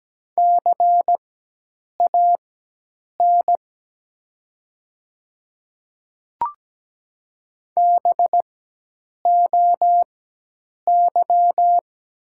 Morse code, heard as a keyed tone, about 17 wpm, spaced as for 8 wpm, 700 Hz. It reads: CAN BOY
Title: 4x Character Spacing - Top 200 Words - 17wpm